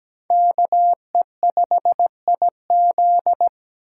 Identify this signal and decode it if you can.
Morse code, heard as a keyed tone, about 20 wpm, 700 Hz. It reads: KE5IZ
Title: Speed Racing - USA 2x2 Call Signs 17wpm